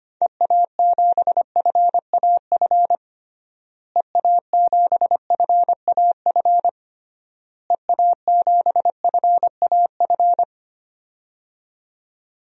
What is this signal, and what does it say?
Morse code, heard as a keyed tone, about 25 wpm, 700 Hz. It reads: EA7FAF EA7FAF EA7FAF
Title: DXCC-100 Call Signs - 6 Characters - Repeated 3x - 25wpm